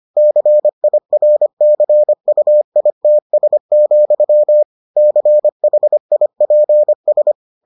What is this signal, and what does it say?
Morse code, heard as a keyed tone, about 25 wpm, 600 Hz. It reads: CIRCUITS, CHIPS